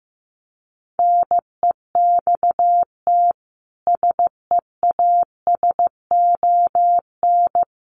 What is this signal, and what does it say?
Morse code, heard as a keyed tone, about 15 wpm, 700 Hz. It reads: NEXT SEASON
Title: Sentences from Top 1400 Words 15wpm